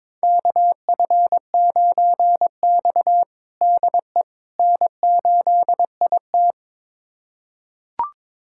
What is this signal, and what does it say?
Morse code, heard as a keyed tone, about 22 wpm, 700 Hz. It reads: KF9X DE N8IT
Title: USA Callsign DE Callsign - 4 Characters - 22wpm